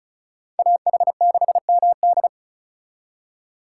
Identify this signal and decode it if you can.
Morse code, heard as a keyed tone, about 43 wpm, 700 Hz. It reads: AH6MD
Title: Speed Racing - USA 2x2 Call Signs 35wpm